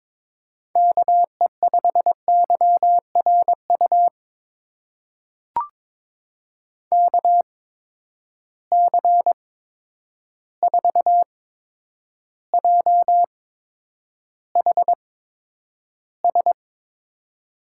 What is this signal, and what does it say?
Morse code, heard as a keyed tone, about 22 wpm, spaced as for 9 wpm, 700 Hz. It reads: KE5YRU KC4JHS
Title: USA Call Signs - 6 Characters - 8x Character Spacing - 22wpm